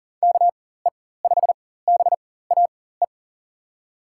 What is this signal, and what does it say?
Morse code, heard as a keyed tone, about 40 wpm, spaced as for 19 wpm, 700 Hz. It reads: KE5BAE
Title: USA Call Signs - 4x Character Spacing - 40wpm